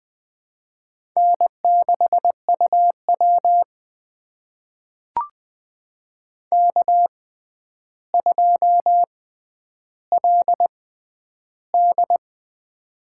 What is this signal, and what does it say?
Morse code, heard as a keyed tone, about 20 wpm, spaced as for 10 wpm, 700 Hz. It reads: N6UW K2LD
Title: USA Call Signs - 4 Characters - 6x Character Spacing - 20wpm